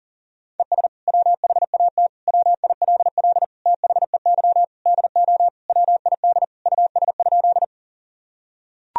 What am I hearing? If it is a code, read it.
Morse code, heard as a keyed tone, about 40 wpm, 700 Hz. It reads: ES WHAT WILL THEY DO WID US?
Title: Sentences from Top 200 Words and Abbreviations 40wpm